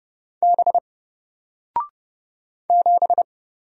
Morse code, heard as a keyed tone, about 30 wpm, 700 Hz. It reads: B 7
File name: Single-Letter-Number-30wpm